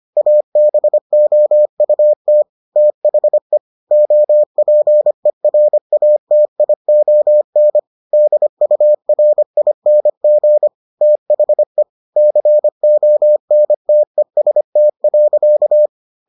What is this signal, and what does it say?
Morse code, heard as a keyed tone, about 25 wpm, 600 Hz. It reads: ABOUT THE OPERATION DURING THE CONTEST.